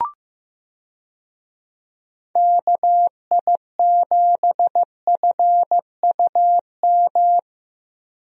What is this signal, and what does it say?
Morse code, heard as a keyed tone, about 15 wpm, 700 Hz. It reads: KI7FUM
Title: USA Call Signs - 6 Characters - Repeated 2x - 15wpm